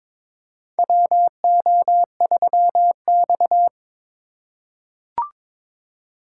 Morse code, heard as a keyed tone, about 22 wpm, 700 Hz. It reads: WO3X
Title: USA Call Signs - 4 Characters - 3x Character Spacing - 22wpm